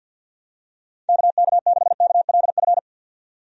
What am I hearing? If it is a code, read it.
Morse code, heard as a keyed tone, about 51 wpm, 700 Hz. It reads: KK6KLF
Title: USA Call Signs - 6 Characters - 50wpm